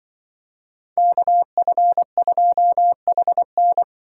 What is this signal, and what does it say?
Morse code, heard as a keyed tone, about 24 wpm, 700 Hz. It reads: KF2HN
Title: Speed Racing - USA 2x2 Call Signs 20wpm